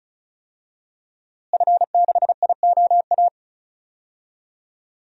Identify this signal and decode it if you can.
Morse code, heard as a keyed tone, about 34 wpm, 700 Hz. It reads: F6IOA
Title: DXCC-100 Call Signs - 5 Characters - Repeated 3x - 35wpm